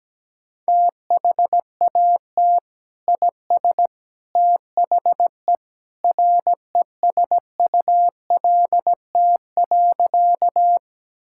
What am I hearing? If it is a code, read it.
Morse code, heard as a keyed tone, about 17 wpm, 700 Hz. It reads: THAT IS THE RESULT.
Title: Sentences from Top 1000 Words - Review Part 5 - 17wpm